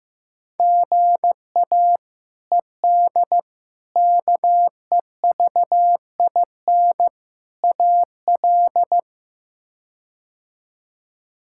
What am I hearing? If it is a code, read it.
Morse code, heard as a keyed tone, about 15 wpm, 700 Hz. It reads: GA ED KEVIN AL
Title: K1USN SST QSO Practice - 15wpm